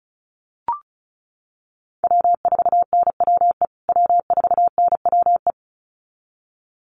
Morse code, heard as a keyed tone, about 34 wpm, 700 Hz. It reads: W4NWE W4NWE
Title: USA Call Signs Repeated 2x 35wpm